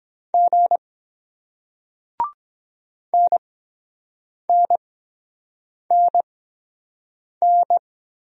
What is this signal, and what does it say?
Morse code, heard as a keyed tone, about 23 wpm, spaced as for 7 wpm, 700 Hz. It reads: G NNNN